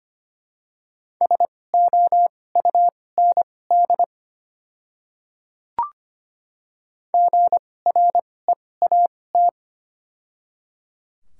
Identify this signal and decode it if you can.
Morse code, heard as a keyed tone, about 25 wpm, spaced as for 18 wpm, 700 Hz. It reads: SOUND GREAT
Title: 2x Character Spacing - Top 200 Words - 25wpm